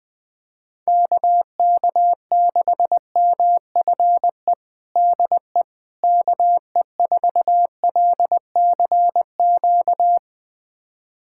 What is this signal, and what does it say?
Morse code, heard as a keyed tone, about 20 wpm, 700 Hz. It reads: KK6MFE DE KE4LCQ
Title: USA Callsign DE Callsign - 6 Characters - 20wpm